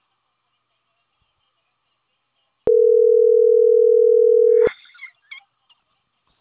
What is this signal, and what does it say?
Ringback tone (North America), 440 + 480 Hz.